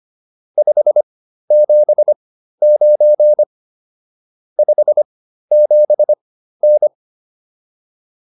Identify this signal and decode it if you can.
Morse code, heard as a keyed tone, about 25 wpm, spaced as for 13 wpm, 600 Hz. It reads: 579 57N
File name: QSO304 13